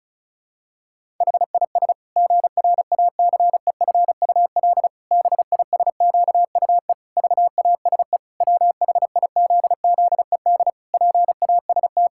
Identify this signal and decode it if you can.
Morse code, heard as a keyed tone, about 34 wpm, 700 Hz. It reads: HIS GRACEFUL BISQUE VASE WHIZZED PAST
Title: Pangram Challenge v2 - 35wpm